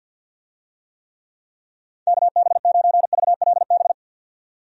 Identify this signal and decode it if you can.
Morse code, heard as a keyed tone, about 48 wpm, 700 Hz. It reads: KB9VLB
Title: USA Call Signs - 6 Characters - 50wpm